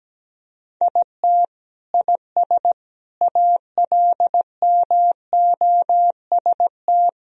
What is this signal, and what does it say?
Morse code, heard as a keyed tone, about 17 wpm, 700 Hz. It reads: IT IS ALMOST
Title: Sentences from Top 1000 Words - Review Part 4 - 17wpm